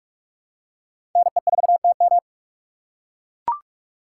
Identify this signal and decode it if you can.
Morse code, heard as a keyed tone, about 44 wpm, 700 Hz. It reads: NE4TM